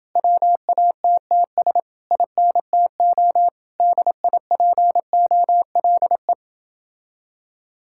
Morse code, heard as a keyed tone, about 28 wpm, 700 Hz. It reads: WATTS INTO DIPOLE